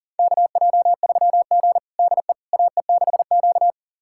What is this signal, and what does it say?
Morse code, heard as a keyed tone, about 40 wpm, 700 Hz. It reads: KJ3G DE AE6Q
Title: USA Callsign DE Callsign - 4 Characters - 40wpm